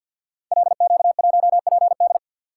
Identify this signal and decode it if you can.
Morse code, heard as a keyed tone, about 52 wpm, 700 Hz. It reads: LQ1PD
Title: Code Groups - 5 Letter:Numbers - 50wpm